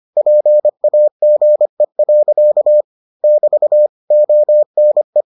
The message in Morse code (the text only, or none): PAGE. <BT> ONE